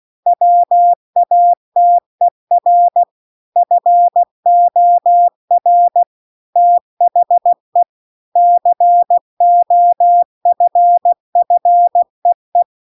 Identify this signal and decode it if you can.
Morse code, heard as a keyed tone, about 16 wpm, 700 Hz. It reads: WATER FOR THE COFFEE